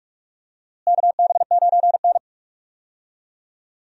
Morse code, heard as a keyed tone, about 46 wpm, 700 Hz. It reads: KB9N
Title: USA Call Signs - 4 Characters - Repeated 2x - 45wpm